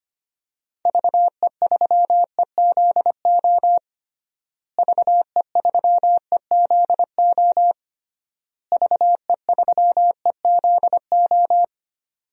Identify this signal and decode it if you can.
Morse code, heard as a keyed tone, about 25 wpm, 700 Hz. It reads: VE3EZO VE3EZO VE3EZO
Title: DXCC-100 Call Signs - 6 Characters - Repeated 3x - 25wpm